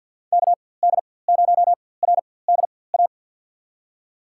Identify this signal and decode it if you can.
Morse code, heard as a keyed tone, about 52 wpm, spaced as for 23 wpm, 700 Hz. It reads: KD0RDA
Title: USA Call Signs - 6 Characters - 4x Character Spacing - 50wpm